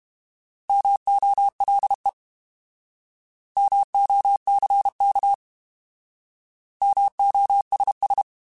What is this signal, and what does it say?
Morse code, heard as a keyed tone, about 32 wpm, 800 Hz. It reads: MOLE MOCK MOSS
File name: Alphabetical / Alphabetical mp3s / Ms @30